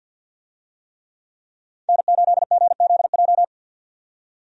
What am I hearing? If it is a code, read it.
Morse code, heard as a keyed tone, about 46 wpm, 700 Hz. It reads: N8GZJ
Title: USA Call Signs - 5 Characters - 50wpm